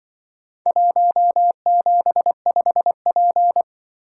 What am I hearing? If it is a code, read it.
Morse code, heard as a keyed tone, about 25 wpm, 700 Hz. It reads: 175P